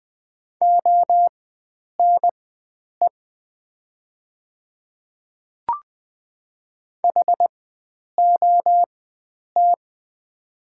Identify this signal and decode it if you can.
Morse code, heard as a keyed tone, about 20 wpm, spaced as for 9 wpm, 700 Hz. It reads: ONE HOT
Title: 4x Character Spacing - Top 300 Words - 20wpm